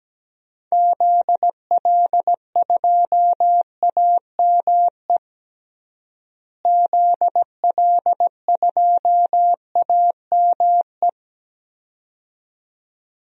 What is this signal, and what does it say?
Morse code, heard as a keyed tone, about 17 wpm, 700 Hz. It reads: ZL2AME ZL2AME
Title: DXCC-100 Call Signs - 6 Characters - Repeated 2x - 17wpm